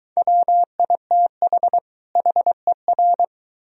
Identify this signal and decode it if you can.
Morse code, heard as a keyed tone, about 23 wpm, 700 Hz. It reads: WITH HER